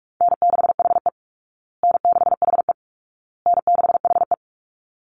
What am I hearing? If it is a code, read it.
Morse code, heard as a keyed tone, about 46 wpm, 700 Hz. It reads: N6HE N6HE N6HE
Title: Easy Call Signs Repeated 3x - 45wpm